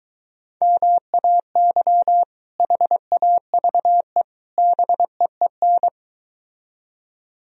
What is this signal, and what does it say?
Morse code, heard as a keyed tone, about 23 wpm, 700 Hz. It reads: MAY HAVE BEEN